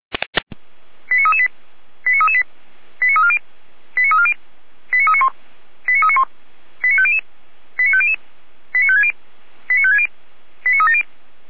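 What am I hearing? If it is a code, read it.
Five-tone selective call (ZVEI), heard as quick runs of five tones, tones of about 80 ms. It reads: 89338 89338 89340 89340 89391 89391 89500 89500 89560 89560 89370
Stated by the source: ZVEI five-tone selective call received on SDR, fire brigade (Feuerwehr) Orlamünde.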